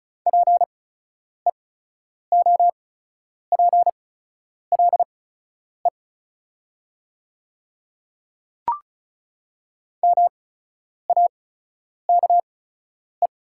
Morse code, heard as a keyed tone, about 36 wpm, spaced as for 10 wpm, 690 Hz. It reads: PEOPLE MAKE
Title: 8x Character Spacing - Top 100 Words - 35wpm